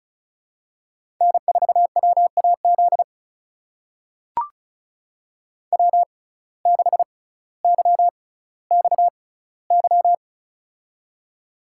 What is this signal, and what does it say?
Morse code, heard as a keyed tone, about 36 wpm, spaced as for 18 wpm, 700 Hz. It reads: N4WAZ W6YXY
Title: USA Call Signs - 6x Character Spacing - 35wpm